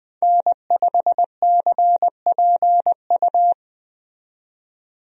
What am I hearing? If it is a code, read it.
Morse code, heard as a keyed tone, about 20 wpm, 700 Hz. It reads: N5CPU
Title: USA Call Signs - 5 Characters - 20wpm